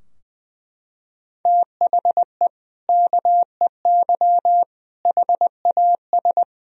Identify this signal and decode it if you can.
Morse code, heard as a keyed tone, about 21 wpm, 700 Hz. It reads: THE KEY HAS
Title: Sentences from Top 1000 Words - Review Part 4 - 20wpm